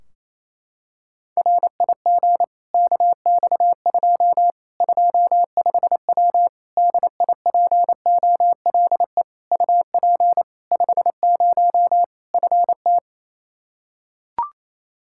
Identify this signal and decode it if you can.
Morse code, heard as a keyed tone, about 27 wpm, 700 Hz. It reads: RIG KX2 25W DIPOLE UP 50 FT
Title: K1USN SST QSO Practice 28wpm